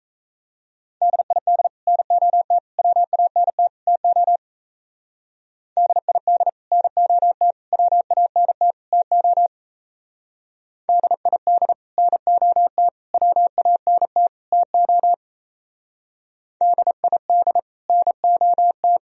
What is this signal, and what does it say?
Morse code, heard as a keyed tone, about 33 wpm, 700 Hz. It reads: DID NOT WANT TO DID NOT WANT TO DID NOT WANT TO DID NOT
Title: Speed-Racing - Sets of 4 Words 28wpm